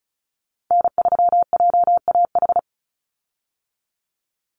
Morse code, heard as a keyed tone, about 34 wpm, 700 Hz. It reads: N3JAH
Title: USA Call Signs 35wpm